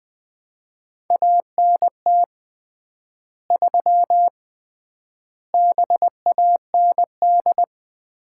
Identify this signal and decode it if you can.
Morse code, heard as a keyed tone, about 20 wpm, 700 Hz. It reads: ANT 3 BAND
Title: Common QSO Phrases - 3x Word-Spacing - 20wpm